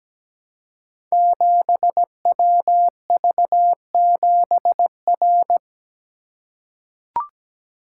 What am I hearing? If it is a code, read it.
Morse code, heard as a keyed tone, about 17 wpm, 700 Hz. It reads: 7WV7R
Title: Code Groups - 5 Letter:Numbers - 17wpm